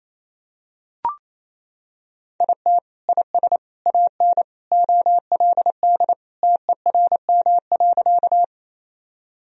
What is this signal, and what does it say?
Morse code, heard as a keyed tone, about 28 wpm, 700 Hz. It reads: IT IS AN OLD TERM.
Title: Sentences from Top 1000 Words - Review Part 4 - 28wpm